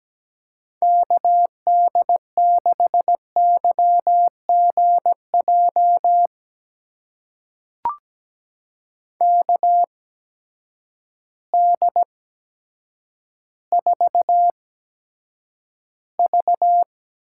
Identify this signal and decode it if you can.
Morse code, heard as a keyed tone, about 17 wpm, spaced as for 8 wpm, 700 Hz. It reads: KD6YGJ KD4V
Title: USA Call Signs - 8x Character Spacing - 17wpm